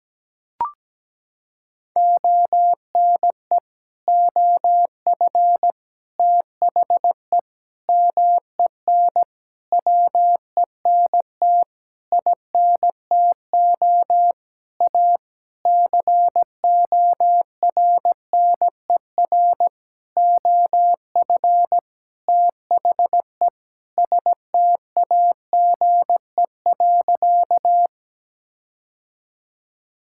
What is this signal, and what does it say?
Morse code, heard as a keyed tone, about 17 wpm, 700 Hz. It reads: ONE OF THE MEN WENT INTO A CORNER OF THE STAGE.
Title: Sentences from Top 1600 Words 17wpm